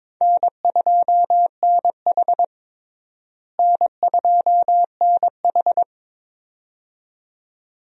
Morse code, heard as a keyed tone, about 22 wpm, 700 Hz. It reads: N2NH N2NH
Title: USA Call Signs - 4 Characters - Repeated 2x - 22wpm